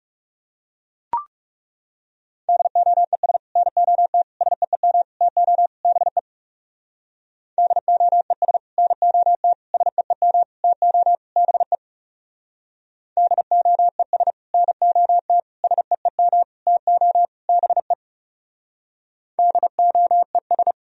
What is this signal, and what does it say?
Morse code, heard as a keyed tone, about 38 wpm, 700 Hz. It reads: DOES NOT SEEM TO BE DOES NOT SEEM TO BE DOES NOT SEEM TO BE DOES